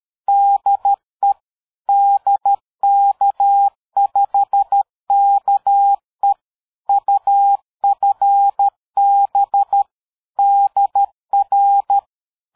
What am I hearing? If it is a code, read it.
Morse code, heard as a keyed tone, about 13 wpm, 800 Hz. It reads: DE DK5KE UFB DR
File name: dk5ke / QSO 60BpM